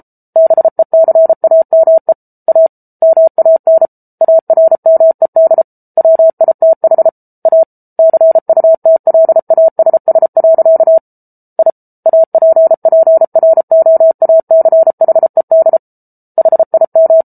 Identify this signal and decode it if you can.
Morse code, heard as a keyed tone, about 34 wpm, 650 Hz. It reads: BECAME A MAN ARMED WITH A CUTLASS. I APPROACHED HIM